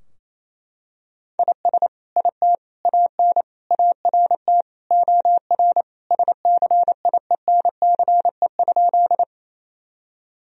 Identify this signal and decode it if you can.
Morse code, heard as a keyed tone, about 28 wpm, 700 Hz. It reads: IS IT AN ART OR SCIENCE?
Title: Sentences from Top 1000 Words - Review Part 3 - 28wpm